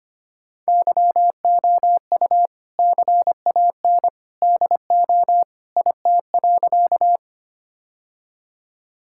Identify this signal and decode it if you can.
Morse code, heard as a keyed tone, about 25 wpm, 700 Hz. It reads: YOU CAN DO IT.
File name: practice / 25wpm / Sentences with 2-Word Phrases - 25wpm